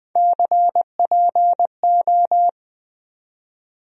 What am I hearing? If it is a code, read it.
Morse code, heard as a keyed tone, about 20 wpm, 700 Hz. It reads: CPO
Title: USA Call Sign Postfixes - 3 Letters 20wpm